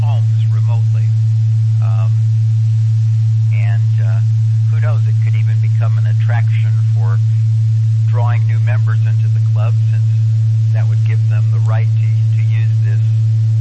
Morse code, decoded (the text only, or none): none